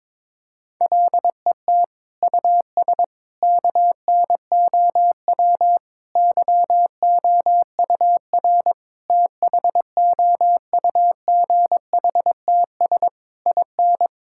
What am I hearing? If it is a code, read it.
Morse code, heard as a keyed tone, about 22 wpm, 700 Hz. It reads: LET US KNOW YOUR THOUGHTS IN
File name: practice / 22wpm / Sets of 8 Words - 22wpm